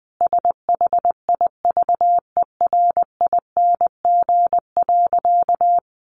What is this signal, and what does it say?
Morse code, heard as a keyed tone, about 20 wpm, 700 Hz. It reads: SHIVERING.
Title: A Sound Of Thunder 20wpm